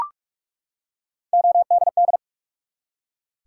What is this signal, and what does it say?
Morse code, heard as a keyed tone, about 44 wpm, 700 Hz. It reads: ODD
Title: Three Letter Words - 45wpm